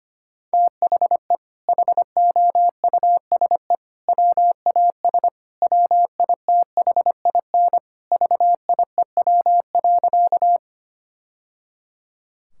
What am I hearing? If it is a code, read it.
Morse code, heard as a keyed tone, about 25 wpm, 700 Hz. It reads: THE HOUSE WAS WITHIN VIEW.
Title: Sentences from Top 1000 Words - Review Part 6 - 25wpm